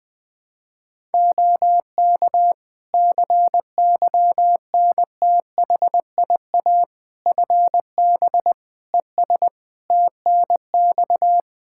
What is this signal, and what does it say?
Morse code, heard as a keyed tone, about 20 wpm, 700 Hz. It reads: OK CYNTHIA FB ES TNX
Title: K1USN SST QSO Practice 20wpm